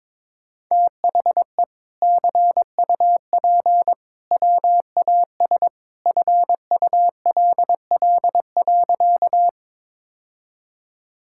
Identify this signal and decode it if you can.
Morse code, heard as a keyed tone, about 22 wpm, 700 Hz. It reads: THE CUP WAS FULL.
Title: Sentences from Top 1600 Words 22wpm